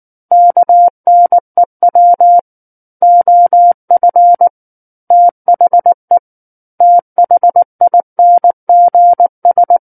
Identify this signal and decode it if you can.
Morse code, heard as a keyed tone, about 19 wpm, 700 Hz. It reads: KNEW OF THE THINGS
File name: Book One / 10-Chapter 10 - 19 WPM